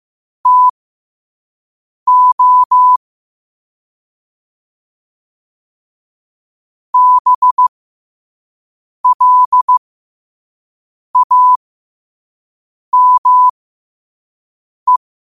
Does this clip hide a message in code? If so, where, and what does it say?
Yes: Morse code from 0.5 to 15.0 s, heard as a keyed tone, about 15 wpm, spaced as for 5 wpm, 1000 Hz. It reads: TO BLAME